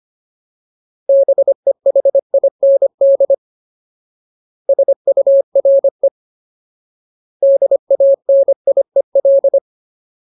Morse code, heard as a keyed tone, about 25 wpm, 550 Hz. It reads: BEHIND SURE DANIEL
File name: Ultimate Word List / 25WPM Ultimate